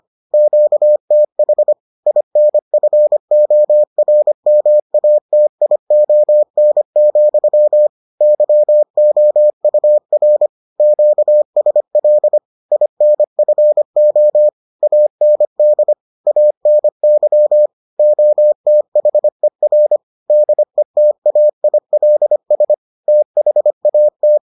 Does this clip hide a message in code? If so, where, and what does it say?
Yes: Morse code from 0.3 to 24.4 s, heard as a keyed tone, about 25 wpm, 600 Hz. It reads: QTH INFORMATION, YOUR QSL INFO AND ANY OTHER DETAILS THAT